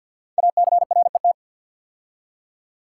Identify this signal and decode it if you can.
Morse code, heard as a keyed tone, about 50 wpm, 690 Hz. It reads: ACRET
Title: Code Groups - 5 Letters - 50wpm